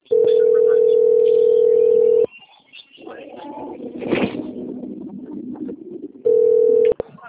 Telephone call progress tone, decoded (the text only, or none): ringback tone